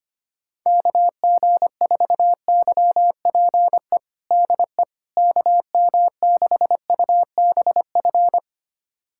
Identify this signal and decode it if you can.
Morse code, heard as a keyed tone, about 25 wpm, 700 Hz. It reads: KG4YPE DE KM6UBF
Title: USA Callsign DE Callsign 25wpm